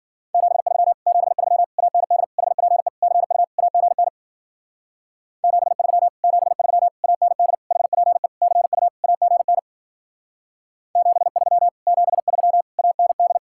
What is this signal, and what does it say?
Morse code, heard as a keyed tone, about 48 wpm, 700 Hz. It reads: 73 73 AND HPE CU AGN 73 73 AND HPE CU AGN 73 73 AND
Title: Speed-Racing Common QSO Phrases Encore - 40wpm